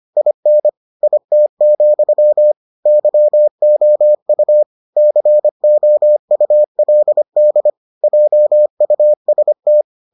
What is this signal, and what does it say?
Morse code, heard as a keyed tone, about 25 wpm, 600 Hz. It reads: IN IT, YOU COULD JUST